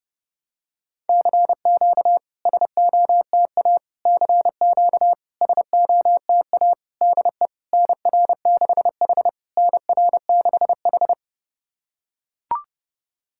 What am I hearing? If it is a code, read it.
Morse code, heard as a keyed tone, about 32 wpm, 690 Hz. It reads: CQ SOTA CQ SOTA DE NR6H NR6H
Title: SOTA QSO Practice 30wpm